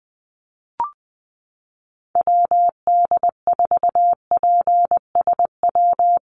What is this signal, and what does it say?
Morse code, heard as a keyed tone, about 20 wpm, 700 Hz. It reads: WD4PSW
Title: USA Call Signs 20wpm